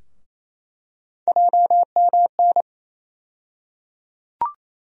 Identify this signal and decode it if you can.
Morse code, heard as a keyed tone, about 28 wpm, 700 Hz. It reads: JMN
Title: USA Call Sign Postfixes - 3 Letters 28wpm